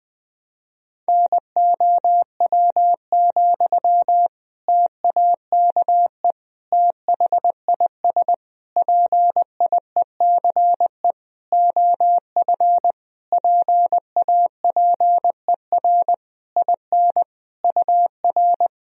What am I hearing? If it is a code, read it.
Morse code, heard as a keyed tone, about 21 wpm, 700 Hz. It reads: NOW, TAKE THIS PIECE OF PAPER IN UR
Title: Sentences from Top 400 Words and Abbreviations 20wpm